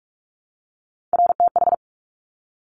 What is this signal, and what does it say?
Morse code, heard as a keyed tone, about 46 wpm, 700 Hz. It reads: RTH
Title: Top Three Letter Combinations in Words 45wpm